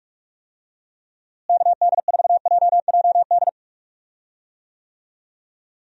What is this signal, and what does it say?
Morse code, heard as a keyed tone, about 48 wpm, 700 Hz. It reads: KD4JJD
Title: USA Call Signs - 6 Characters - Repeated 2x - 45wpm